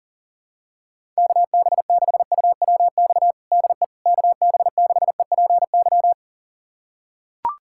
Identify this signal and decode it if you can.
Morse code, heard as a keyed tone, about 40 wpm, 700 Hz. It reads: KB6UWX DE KB6EPY
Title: USA Callsign DE Callsign 40wpm